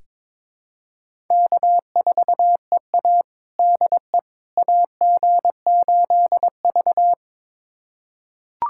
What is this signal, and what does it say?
Morse code, heard as a keyed tone, about 22 wpm, 700 Hz. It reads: K4EA DE AG8V
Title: USA Callsign DE Callsign - 4 Characters - 22wpm